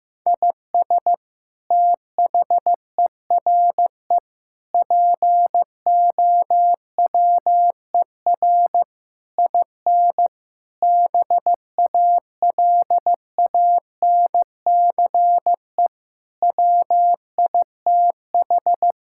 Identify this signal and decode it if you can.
Morse code, heard as a keyed tone, about 15 wpm, 700 Hz. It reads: IS THERE POWER IN BALANCE WITH
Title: Sentences from Top 1000 Words - Review Part 6 - 15wpm